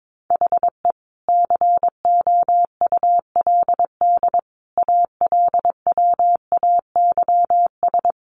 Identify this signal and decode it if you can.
Morse code, heard as a keyed tone, about 22 wpm, 700 Hz. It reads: HE COULD ALWAYS
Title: Sentences from Top 1400 Words 22wpm